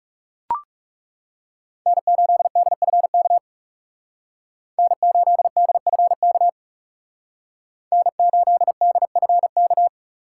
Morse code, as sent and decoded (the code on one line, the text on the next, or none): -. ---.. -.. ..-. -.- / -. ---.. -.. ..-. -.- / -. ---.. -.. ..-. -.-
N8DFK N8DFK N8DFK